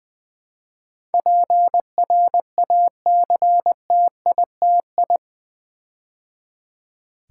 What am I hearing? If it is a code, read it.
Morse code, heard as a keyed tone, about 20 wpm, 700 Hz. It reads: PRACTITI
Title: Build A Word - 4 Letters At A Time - 12-Letter Words - 20wpm